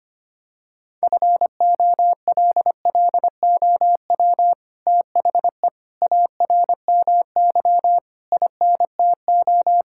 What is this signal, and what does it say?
Morse code, heard as a keyed tone, about 25 wpm, 700 Hz. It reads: FOLLOW THE ARMY INTO
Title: Sentences from Top 1000 Words - Review Part 5 - 25wpm